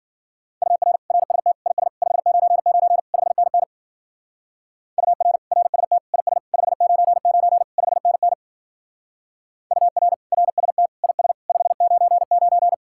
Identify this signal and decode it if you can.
Morse code, heard as a keyed tone, about 51 wpm, 700 Hz. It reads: UR RST IS 599 5NN UR RST IS 599 5NN UR RST IS 599